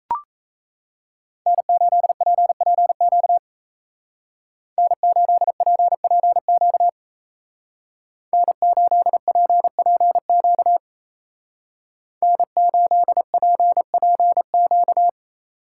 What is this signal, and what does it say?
Morse code, heard as a keyed tone, about 34 wpm, 700 Hz. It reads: N8PPQ N8PPQ N8PPQ N8PPQ
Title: Speed Racing - USA 1x3 Call Signs 28wpm